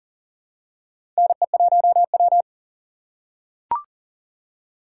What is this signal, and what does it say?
Morse code, heard as a keyed tone, about 40 wpm, 700 Hz. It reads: NE1W
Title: USA Call Signs - 4 Characters - 4x Character Spacing - 40wpm